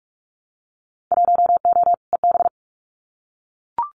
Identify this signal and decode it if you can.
Morse code, heard as a keyed tone, about 44 wpm, 700 Hz. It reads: 1O EB
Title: Winter Field Day Exchanges 45wpm